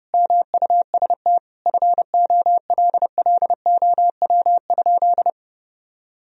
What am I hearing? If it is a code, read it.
Morse code, heard as a keyed tone, about 30 wpm, 700 Hz. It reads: MUST FOLLOW?
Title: Sentences from Top 1000 Words - Review Part 2 - 30wpm